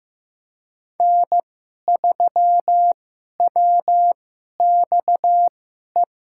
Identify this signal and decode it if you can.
Morse code, heard as a keyed tone, about 15 wpm, spaced as for 11 wpm, 700 Hz. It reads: N3WXE
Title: USA Call Signs - 5 Characters - 2x Character Spacing - 15wpm